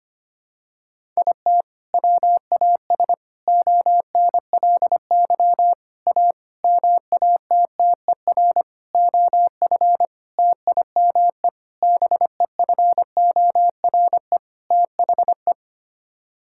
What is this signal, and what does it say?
Morse code, heard as a keyed tone, about 25 wpm, 700 Hz. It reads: IT WAS ONLY A MATTER OF TIME BEFORE THE
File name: Sets of 9 Words - 25wpm